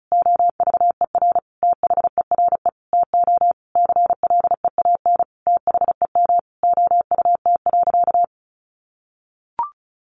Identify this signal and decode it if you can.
Morse code, heard as a keyed tone, about 34 wpm, 700 Hz. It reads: OVER THERE TO CLEAN THEM OUT.
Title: Sentences from Top 1500 Words 35wpm